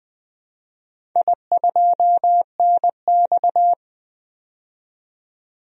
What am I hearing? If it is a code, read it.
Morse code, heard as a keyed tone, about 20 wpm, 700 Hz. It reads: I2NX
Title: DXCC-100 Call Signs - 4 Characters - 20wpm